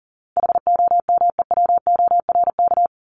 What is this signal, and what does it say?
Morse code, heard as a keyed tone, about 40 wpm, 700 Hz. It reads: HOMEWORK
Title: Top 1501-1750 Words 40wpm